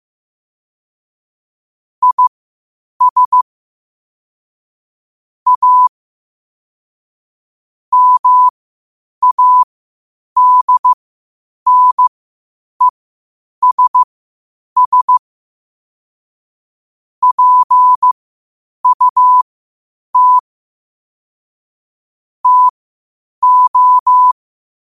Morse code, heard as a keyed tone, about 15 wpm, spaced as for 8 wpm, 1000 Hz. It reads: IS A MADNESS PUT TO
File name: qotdfiles / 8wpm